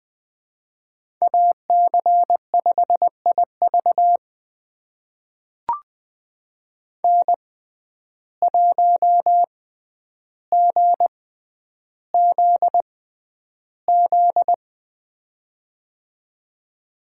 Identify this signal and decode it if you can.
Morse code, heard as a keyed tone, about 20 wpm, spaced as for 10 wpm, 700 Hz. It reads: AC5IV N1GZZ
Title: USA Call Signs - 5 Characters - 6x Character Spacing - 20wpm